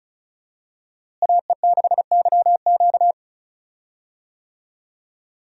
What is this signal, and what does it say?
Morse code, heard as a keyed tone, about 36 wpm, 700 Hz. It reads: AE6YQ
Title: USA Call Signs - 5 Characters - Repeated 3x - 35wpm